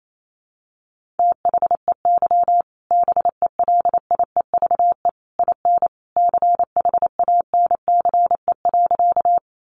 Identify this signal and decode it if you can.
Morse code, heard as a keyed tone, about 28 wpm, 700 Hz. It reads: THEY BELIEVE IN CHANCE.
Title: Sentences from Top 1000 Words 28wpm